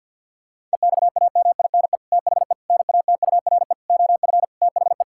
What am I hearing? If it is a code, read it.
Morse code, heard as a keyed tone, about 50 wpm, 700 Hz. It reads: EXAMINE THE NATURE OF THE
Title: Sentences from Top 1600 Words 50wpm